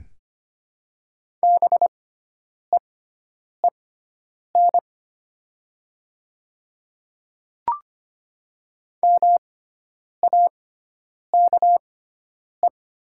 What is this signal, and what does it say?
Morse code, heard as a keyed tone, about 25 wpm, spaced as for 9 wpm, 700 Hz. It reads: BEEN MAKE
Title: 6x Character Spacing - Top 100 Words - 25wpm